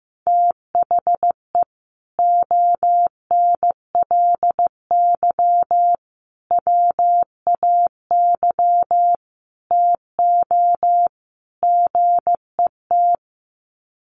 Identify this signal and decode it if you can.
Morse code, heard as a keyed tone, about 15 wpm, 700 Hz. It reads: THE ONLY WAY TO GET
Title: Sets of 5 Words Encore2 15wpm